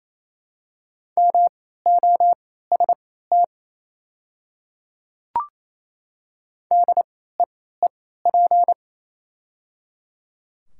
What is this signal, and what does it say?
Morse code, heard as a keyed tone, about 28 wpm, spaced as for 16 wpm, 700 Hz. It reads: MOST DEEP
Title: 3x Character Spacing - Top 500 Words - 28wpm